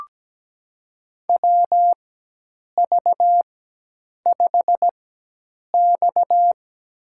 Morse code, heard as a keyed tone, about 17 wpm, spaced as for 8 wpm, 700 Hz. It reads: WV5X
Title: USA Call Signs - 4x Character Spacing - 17wpm